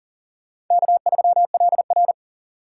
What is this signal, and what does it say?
Morse code, heard as a keyed tone, about 40 wpm, 700 Hz. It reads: K3LR
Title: K1USN SST QSO Practice - 40wpm